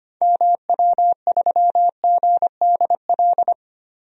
Morse code, heard as a keyed tone, about 25 wpm, 700 Hz. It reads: MW3GDL
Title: DXCC-100 Call Signs - 6 Characters - 25wpm